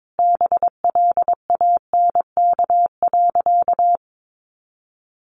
Morse code, heard as a keyed tone, about 22 wpm, 700 Hz. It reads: BLANK.
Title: Sentences from Top 1500 Words 22wpm